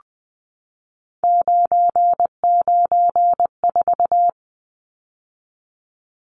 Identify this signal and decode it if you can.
Morse code, heard as a keyed tone, about 20 wpm, 700 Hz. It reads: 994
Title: Numbers - 3-Digit 20wpm